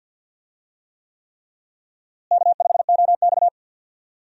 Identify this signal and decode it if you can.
Morse code, heard as a keyed tone, about 51 wpm, 700 Hz. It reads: K5OX
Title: USA Call Signs - 4 Characters - Repeated 3x - 50wpm